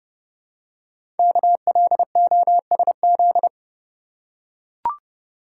Morse code, heard as a keyed tone, about 30 wpm, 700 Hz. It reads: KLOSZ
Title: Code Groups - 5 Letters - 30wpm